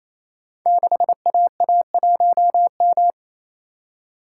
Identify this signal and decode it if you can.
Morse code, heard as a keyed tone, about 28 wpm, 700 Hz. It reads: 6AA1M
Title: Code Groups - 5 Letter:Numbers - 28wpm